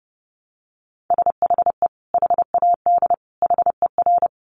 Morse code, heard as a keyed tone, about 30 wpm, 700 Hz. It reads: SHE HAD HER
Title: Sentences from Top 900 Words 30wpm